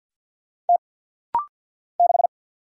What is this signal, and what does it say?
Morse code, heard as a keyed tone, about 50 wpm, 700 Hz. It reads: T 6
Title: Single Letter-Number Alt - Rapid-Fire 50wpm